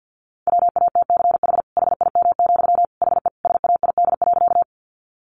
Morse code, heard as a keyed tone, about 50 wpm, 700 Hz. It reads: WATCH HIM, HE SAID.
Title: Sentences from Top 300 Words 50wpm